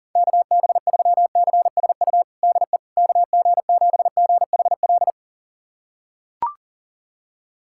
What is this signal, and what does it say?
Morse code, heard as a keyed tone, about 40 wpm, 700 Hz. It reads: KB3CSU DE KG7GHL
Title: USA Callsign DE Callsign - 6 Characters - 40wpm